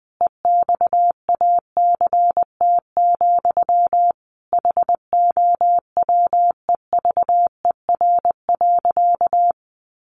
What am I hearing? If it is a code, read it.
Morse code, heard as a keyed tone, about 20 wpm, 700 Hz. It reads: EXACT, HOWEVER.
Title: Sentences from Top 1300 Words 20wpm